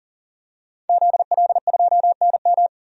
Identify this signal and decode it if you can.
Morse code, heard as a keyed tone, about 37 wpm, 700 Hz. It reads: ZL2NM
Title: DXCC-100 Call Signs - 5 Characters - 40wpm